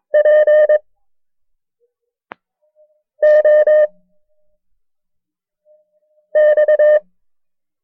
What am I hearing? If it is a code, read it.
Morse code, heard as a keyed tone, about 22 wpm, 610 Hz. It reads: P O X
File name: FreeFiles / CW-practice / KB6MT-tape1-side2